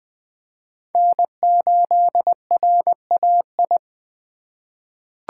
Morse code, heard as a keyed tone, about 20 wpm, 700 Hz. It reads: N8RAI
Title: USA Call Signs - 5 Characters - 2x Character Spacing - 20wpm